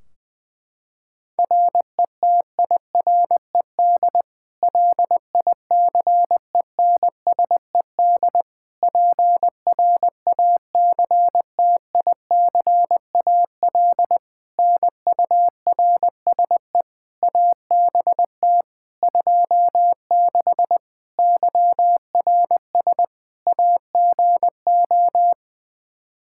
Morse code, heard as a keyed tone, about 20 wpm, 700 Hz. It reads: RETIRED LICENSED PRACTICAL NURSE ABT 26 YRS AGO
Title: Practice QSO - Exchange 3 - More Info - 20wpm